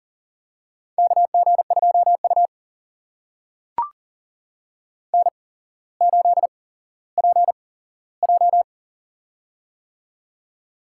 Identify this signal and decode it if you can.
Morse code, heard as a keyed tone, about 40 wpm, spaced as for 17 wpm, 700 Hz. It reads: KG2U N8PJ
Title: USA Call Signs - 4 Characters - 8x Character Spacing - 40wpm